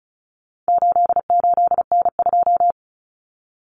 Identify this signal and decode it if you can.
Morse code, heard as a keyed tone, about 34 wpm, 700 Hz. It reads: 88N2
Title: Common Cut Numbers - 4-Digit 35wpm